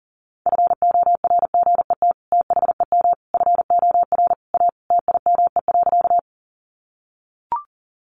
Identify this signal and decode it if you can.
Morse code, heard as a keyed tone, about 40 wpm, 700 Hz. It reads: FORGET THEM FOR A TIME.
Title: Sentences from Top 1200 Words 40wpm